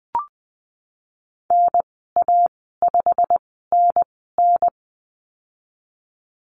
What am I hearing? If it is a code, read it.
Morse code, heard as a keyed tone, about 20 wpm, spaced as for 15 wpm, 700 Hz. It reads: NA5NN
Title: 088 Lesson 15g - Number 5 - Call Signs 20wpm